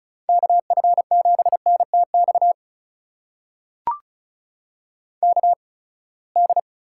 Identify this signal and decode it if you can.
Morse code, heard as a keyed tone, about 36 wpm, spaced as for 24 wpm, 700 Hz. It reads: KF7NTX KD
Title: USA Call Signs - 6 Characters - 8x Character Spacing - 35wpm